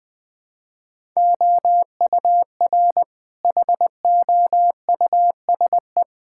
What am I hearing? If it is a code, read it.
Morse code, heard as a keyed tone, about 20 wpm, 700 Hz. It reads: OUR HOUSE
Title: Sentences with 5-Word Phrases - Spoken Beginning - 20wpm